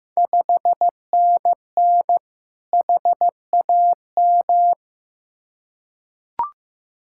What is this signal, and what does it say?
Morse code, heard as a keyed tone, about 15 wpm, 700 Hz. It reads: 5NN HAM
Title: Ontario QSO Party Exchange 15wpm